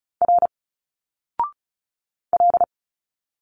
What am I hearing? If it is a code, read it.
Morse code, heard as a keyed tone, about 35 wpm, 700 Hz. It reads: R L
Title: Single Letters 35wpm